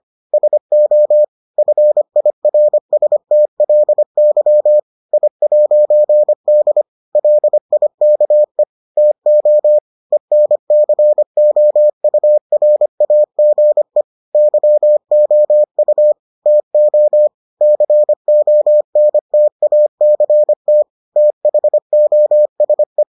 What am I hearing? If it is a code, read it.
Morse code, heard as a keyed tone, about 25 wpm, 600 Hz. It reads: SO FIRSTLY I'D LIKE TO ENCOURAGE YOU TO CONTACT THOSE